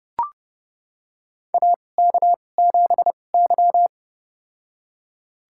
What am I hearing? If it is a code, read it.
Morse code, heard as a keyed tone, about 30 wpm, spaced as for 22 wpm, 700 Hz. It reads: AK7Y
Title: USA Call Signs - 4 Characters - 2x Character Spacing - 30wpm